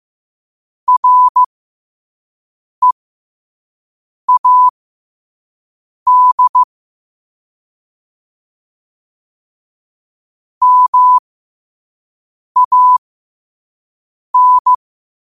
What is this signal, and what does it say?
Morse code, heard as a keyed tone, about 15 wpm, spaced as for 5 wpm, 1000 Hz. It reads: READ MAN
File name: qotdfiles / 5wpm